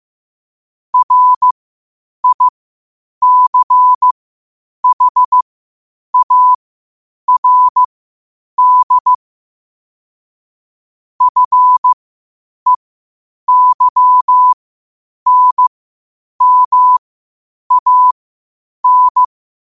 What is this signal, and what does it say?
Morse code, heard as a keyed tone, about 15 wpm, spaced as for 8 wpm, 1000 Hz. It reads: RICHARD FEYNMAN
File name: qotdfiles / 8wpm